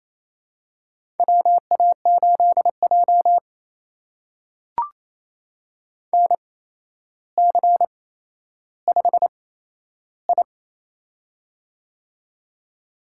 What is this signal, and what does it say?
Morse code, heard as a keyed tone, about 27 wpm, spaced as for 12 wpm, 700 Hz. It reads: WA8J NC5I
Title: USA Call Signs - 4 Characters - 8x Character Spacing - 28wpm